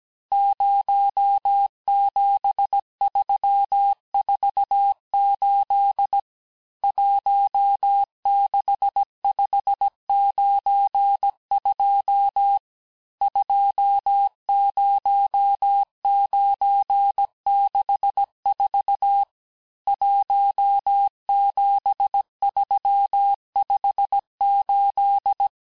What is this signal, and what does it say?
Morse code, heard as a keyed tone, about 17 wpm, 800 Hz. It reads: 07348 16592 20964 17358
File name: cours-du-lundi files / cours-1587-texte-chiffre-vitesse-16MPM